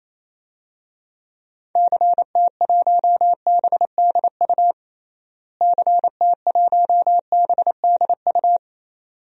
Morse code, heard as a keyed tone, about 28 wpm, 700 Hz. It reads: CT1BDU CT1BDU
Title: DXCC-100 Call Signs - 6 Characters - Repeated 2x - 28wpm